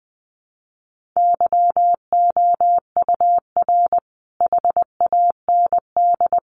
Morse code, heard as a keyed tone, about 20 wpm, 700 Hz. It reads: YOUR HAND
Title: Sentences from Top 1500 Words 20wpm